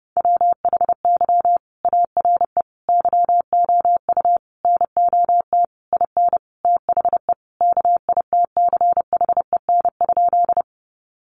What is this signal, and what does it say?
Morse code, heard as a keyed tone, about 30 wpm, 700 Hz. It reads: WHY ARE YOU NOT IN THE KITCHEN?
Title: Sentences from Top 1400 Words 30wpm